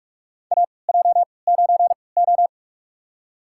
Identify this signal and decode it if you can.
Morse code, heard as a keyed tone, about 46 wpm, spaced as for 26 wpm, 690 Hz. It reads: AJ9O
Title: USA Call Signs - 4 Characters - 3x Character Spacing - 45wpm